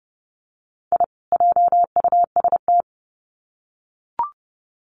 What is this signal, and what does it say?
Morse code, heard as a keyed tone, about 30 wpm, 700 Hz. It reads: I JUST
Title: Sets of 2 Words 30wpm